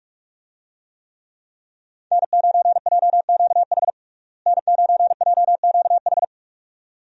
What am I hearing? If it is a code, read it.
Morse code, heard as a keyed tone, about 46 wpm, 700 Hz. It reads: N9JQH N9JQH
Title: USA Call Signs - 5 Characters - Repeated 2x - 45wpm